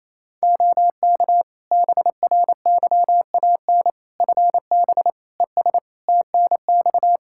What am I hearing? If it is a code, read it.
Morse code, heard as a keyed tone, about 27 wpm, 700 Hz. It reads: OK BRYAN FB ES TNX
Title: Practice QSO - Exchange 2 - Info - 28wpm